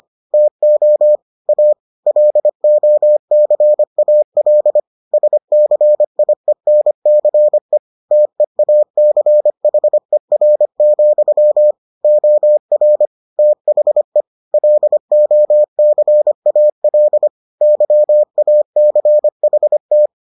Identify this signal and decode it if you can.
Morse code, heard as a keyed tone, about 25 wpm, 600 Hz. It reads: TO A LOCAL SCIENCE TEACHER, OR THE LOCAL YACHT